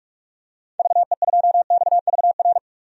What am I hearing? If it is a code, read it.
Morse code, heard as a keyed tone, about 47 wpm, 690 Hz. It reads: VE2XVR
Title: DXCC-100 Call Signs - 6 Characters - 45wpm